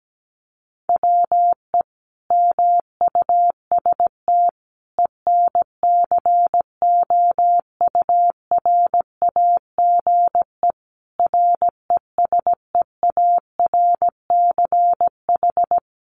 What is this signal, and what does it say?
Morse code, heard as a keyed tone, about 17 wpm, 700 Hz. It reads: WE MUST ENCOURAGE RESEARCH
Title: Sentences from Top 1400 Words 17wpm